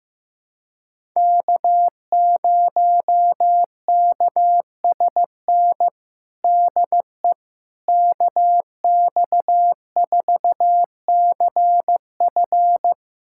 Morse code, heard as a keyed tone, about 15 wpm, 700 Hz. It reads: K0KSN DE KX4CF
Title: USA Callsign DE Callsign 15wpm